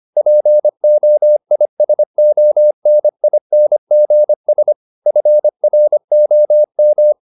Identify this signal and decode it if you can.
Morse code, heard as a keyed tone, about 25 wpm, 600 Hz. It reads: POISONINGS FROM
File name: the old man and the sea part 1 25wpm